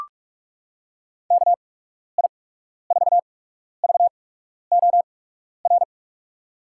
Morse code, heard as a keyed tone, about 44 wpm, spaced as for 12 wpm, 700 Hz. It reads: KI4VOR
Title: USA Call Signs - 6 Characters - 8x Character Spacing - 45wpm